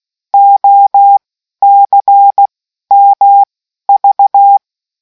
Morse code, heard as a keyed tone, about 16 wpm, spaced as for 12 wpm, 800 Hz. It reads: OCMV